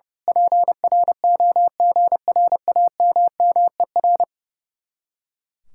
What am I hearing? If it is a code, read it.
Morse code, heard as a keyed tone, about 28 wpm, 700 Hz. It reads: PROGRAMMER
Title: Speed-Racing Common QSO Phrases - 30wpm